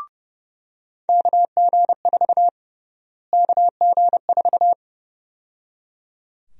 Morse code, heard as a keyed tone, about 29 wpm, 700 Hz. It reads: KG4 KG4
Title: USA Call Sign Prefixes - Repeated 2x - 30wpm